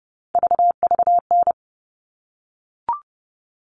Morse code, heard as a keyed tone, about 30 wpm, 700 Hz. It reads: VVN
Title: All Cut Numbers - 3-Digit 30wpm